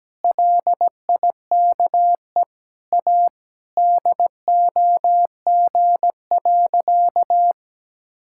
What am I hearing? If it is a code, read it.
Morse code, heard as a keyed tone, about 17 wpm, 700 Hz. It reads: LIKE A DOG.
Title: Sentences from Top 1600 Words 17wpm